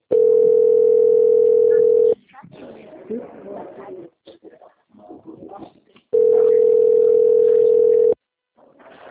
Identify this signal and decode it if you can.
Ringback tone (North America), 440 + 480 Hz, with speech.
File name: out- - - - - .35507